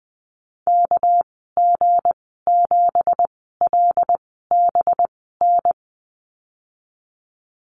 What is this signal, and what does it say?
Morse code, heard as a keyed tone, about 20 wpm, spaced as for 15 wpm, 700 Hz. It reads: KG7LBN
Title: 219 Lesson 35g - Number 0 - Call Signs 20wpm